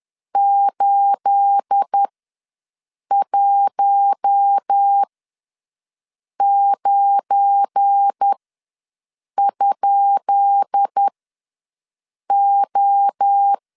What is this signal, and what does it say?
Morse code, heard as a keyed tone, about 10 wpm, spaced as for 5 wpm, 800 Hz. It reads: 819?O
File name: LEZIONI / 10 SIGNS / 10B INT ws 10wpm 4134